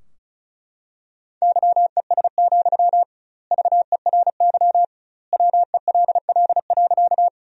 Morse code, heard as a keyed tone, about 34 wpm, 700 Hz. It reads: YES, VERY WELL.
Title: 2x Word-Spacing - Sentences from Top 500 Words - 35wpm